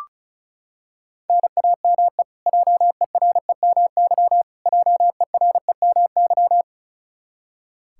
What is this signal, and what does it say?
Morse code, heard as a keyed tone, about 34 wpm, 700 Hz. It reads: NAME JEREMY JEREMY
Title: Practice QSO - Exchange 1 - Beginning - 35wpm